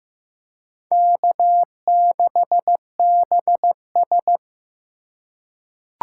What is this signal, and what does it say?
Morse code, heard as a keyed tone, about 15 wpm, 700 Hz. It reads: K6BS
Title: USA Call Signs - 4 Characters - 4x Character Spacing - 15wpm